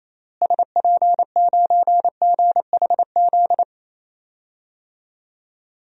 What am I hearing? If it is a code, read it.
Morse code, heard as a keyed tone, about 28 wpm, 700 Hz. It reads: SP9GHZ
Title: DXCC-100 Call Signs - 6 Characters - Repeated 2x - 28wpm